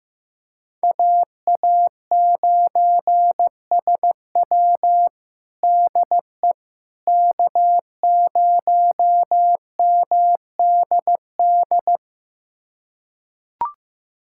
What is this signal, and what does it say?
Morse code, heard as a keyed tone, about 15 wpm, 700 Hz. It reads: AA9SW DE K0MDD
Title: USA Callsign DE Callsign 15wpm